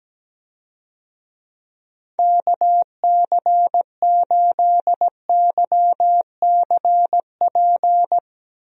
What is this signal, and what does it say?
Morse code, heard as a keyed tone, about 17 wpm, 700 Hz. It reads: KC8YCP
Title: USA Call Signs - 6 Characters - 17wpm